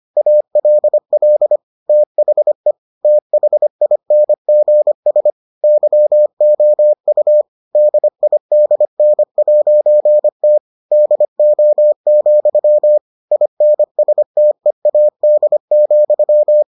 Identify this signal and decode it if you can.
Morse code, heard as a keyed tone, about 25 wpm, 600 Hz. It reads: ALL THE THINGS YOU DIDN'T DO, INSTEAD,